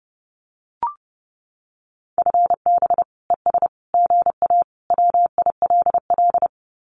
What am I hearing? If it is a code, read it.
Morse code, heard as a keyed tone, about 30 wpm, 700 Hz. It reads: FB ES GA WILL
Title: Common QSO Phrases Encore 30wpm